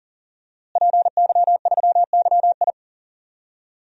Morse code, heard as a keyed tone, about 40 wpm, 700 Hz. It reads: PY3YI
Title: DXCC-100 Call Signs - 5 Characters - Repeated 3x - 40wpm